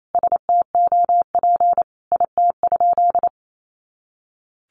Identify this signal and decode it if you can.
Morse code, heard as a keyed tone, about 28 wpm, 700 Hz. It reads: STOP IT?
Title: Sentences from Top 300 Words 28wpm